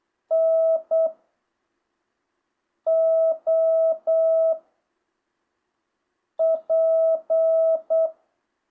Morse code, heard as a keyed tone, about 8 wpm, 660 Hz. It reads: N O P
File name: jh8cbh